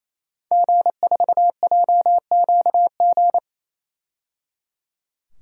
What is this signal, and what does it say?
Morse code, heard as a keyed tone, about 28 wpm, 700 Hz. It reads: G4JQG
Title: DXCC-100 Call Signs - 5 Characters - 28wpm